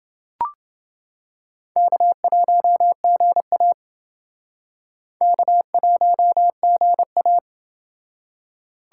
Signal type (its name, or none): Morse code